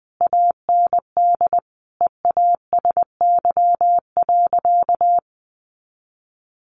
Morse code, heard as a keyed tone, about 20 wpm, 700 Hz. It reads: AND EASY.
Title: Sentences from Top 1500 Words 20wpm